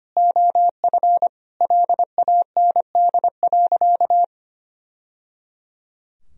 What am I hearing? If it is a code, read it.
Morse code, heard as a keyed tone, about 25 wpm, 700 Hz. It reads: OF LAND.
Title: Sentences from Top 1000 Words - Review Part 2 - 25wpm